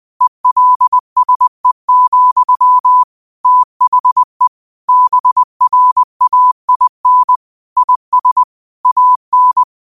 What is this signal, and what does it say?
Morse code, heard as a keyed tone, about 20 wpm, 1000 Hz. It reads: ELSE, THE BRAIN IS AN